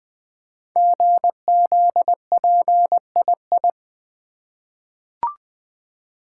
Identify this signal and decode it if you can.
Morse code, heard as a keyed tone, about 20 wpm, 700 Hz. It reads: GZPII
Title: Code Groups - 5 Letter:Numbers - 20wpm